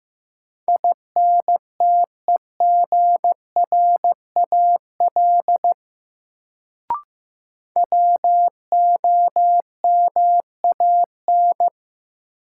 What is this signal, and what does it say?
Morse code, heard as a keyed tone, about 15 wpm, 700 Hz. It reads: INTEGRAL WOMAN
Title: Sets of 5 Words - New Word Drill - 15wpm